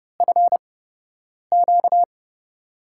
Morse code, heard as a keyed tone, about 30 wpm, spaced as for 8 wpm, 700 Hz. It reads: FQ
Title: USA Call Signs - 4 Characters - 8x Character Spacing - 30wpm